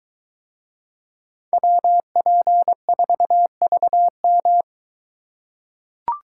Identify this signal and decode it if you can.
Morse code, heard as a keyed tone, about 22 wpm, 700 Hz. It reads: WP4VM